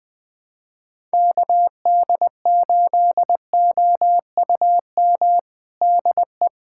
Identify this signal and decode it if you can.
Morse code, heard as a keyed tone, about 20 wpm, 700 Hz. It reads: KD8OUM DE
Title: USA Callsign DE Callsign 20wpm